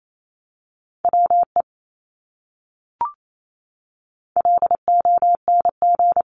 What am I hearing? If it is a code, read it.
Morse code, heard as a keyed tone, about 28 wpm, 700 Hz. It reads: WE LONG
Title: Top 200 Words Repeated 28wpm